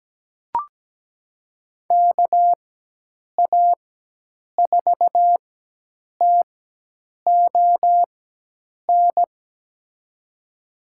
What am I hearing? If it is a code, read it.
Morse code, heard as a keyed tone, about 17 wpm, spaced as for 8 wpm, 700 Hz. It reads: KA4TON
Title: USA Call Signs - 6 Characters - 4x Character Spacing - 17wpm